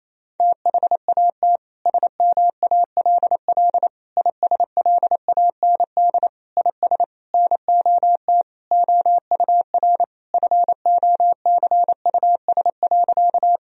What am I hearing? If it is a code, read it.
Morse code, heard as a keyed tone, about 28 wpm, 700 Hz. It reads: THAT SMALL ISLAND IS NOT OUR FOCUS.
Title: Sentences from Top 1000 Words - Review Part 4 - 28wpm